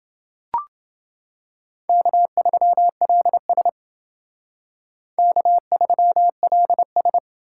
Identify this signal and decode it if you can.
Morse code, heard as a keyed tone, about 28 wpm, 700 Hz. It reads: K3LS K3LS